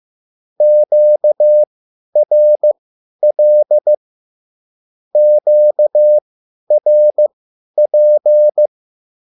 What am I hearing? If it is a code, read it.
Morse code, heard as a keyed tone, about 15 wpm, spaced as for 10 wpm, 600 Hz. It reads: QRL QRP